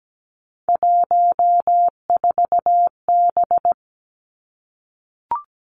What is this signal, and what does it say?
Morse code, heard as a keyed tone, about 17 wpm, 700 Hz. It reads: 14B
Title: Three Challenging Letter-Number Combinations 17wpm